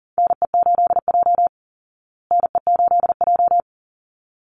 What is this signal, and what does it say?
Morse code, heard as a keyed tone, about 40 wpm, 700 Hz. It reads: NE8J NE8J
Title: Call Signs Repeated 3x - 40wpm